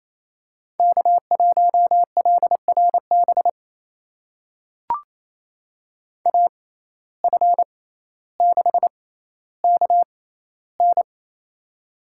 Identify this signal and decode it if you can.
Morse code, heard as a keyed tone, about 28 wpm, spaced as for 14 wpm, 700 Hz. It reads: K1LRB AF6KN
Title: USA Call Signs - 5 Characters - 6x Character Spacing - 28wpm